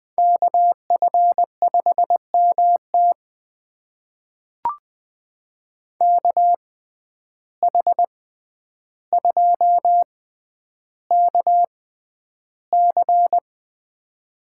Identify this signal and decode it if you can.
Morse code, heard as a keyed tone, about 20 wpm, spaced as for 10 wpm, 700 Hz. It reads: KF5MT KH2KC
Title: USA Call Signs - 5 Characters - 6x Character Spacing - 20wpm